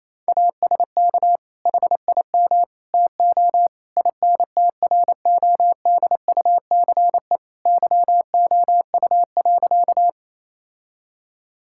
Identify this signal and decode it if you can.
Morse code, heard as a keyed tone, about 28 wpm, 700 Hz. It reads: ASK HIM TO INTRODUCE YOU.
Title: Sentences from Top 1600 Words 28wpm